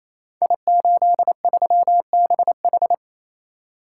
Morse code, heard as a keyed tone, about 28 wpm, 700 Hz. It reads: I83BH
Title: Code Groups - 5 Letter:Numbers - 28wpm